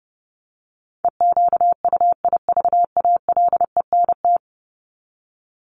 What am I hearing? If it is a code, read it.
Morse code, heard as a keyed tone, about 30 wpm, 700 Hz. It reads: EQUIVALENT
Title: Top 1501-1750 Words 30wpm